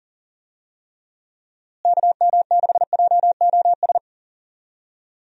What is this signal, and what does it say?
Morse code, heard as a keyed tone, about 40 wpm, 700 Hz. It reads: KM6JOS
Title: USA Call Signs - 6 Characters - Repeated 3x - 40wpm